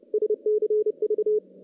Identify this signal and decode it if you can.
Morse code, heard as a keyed tone, about 29 wpm, 420 Hz. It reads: SCV
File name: contests / CW-SS-k5tr- -1300